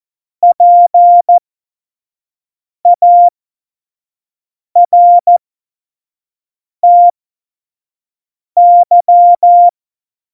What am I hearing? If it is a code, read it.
Morse code, heard as a keyed tone, about 14 wpm, spaced as for 5 wpm, 700 Hz. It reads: PARTY
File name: Morsekurs / Witze / Witz 03. November 2020, 25 F70